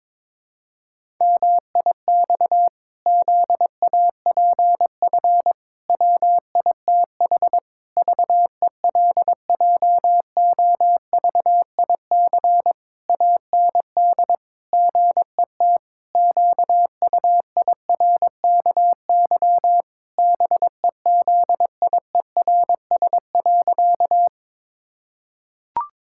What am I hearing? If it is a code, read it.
Morse code, heard as a keyed tone, about 22 wpm, 700 Hz. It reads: MIX ZAPF WITH VELJOVIC AND GET QUIRKY BEZIERS.
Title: Pangram Challenge v2 - 22wpm